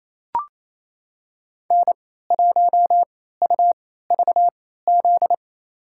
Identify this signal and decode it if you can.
Morse code, heard as a keyed tone, about 28 wpm, spaced as for 16 wpm, 700 Hz. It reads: N1UVZ
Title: USA Call Signs - 3x Character Spacing - 28wpm